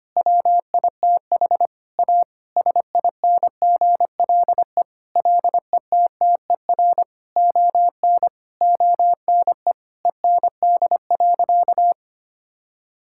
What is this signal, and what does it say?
Morse code, heard as a keyed tone, about 25 wpm, 700 Hz. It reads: WITH A SINGLE LETTER ON ONE END.
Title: Sentences from Top 1600 Words 25wpm